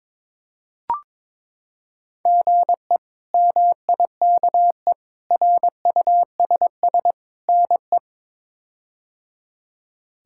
Morse code, heard as a keyed tone, about 22 wpm, 700 Hz. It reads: GE MIKE RUSS NE
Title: K1USN SST QSO Practice - 22wpm